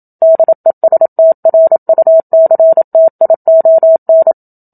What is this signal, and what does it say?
Morse code, heard as a keyed tone, about 27 wpm, 650 Hz. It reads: DESTRUCTION
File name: Book Two / 01-Chapter 1 - 27 WPM